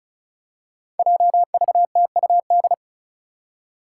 Morse code, heard as a keyed tone, about 36 wpm, 700 Hz. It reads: JVTUD
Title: Five Challenging Letter-Number Combinations 35wpm